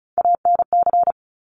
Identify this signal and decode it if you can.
Morse code, heard as a keyed tone, about 36 wpm, 700 Hz. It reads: ANC
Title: Top Three Letter Combinations in Words 35wpm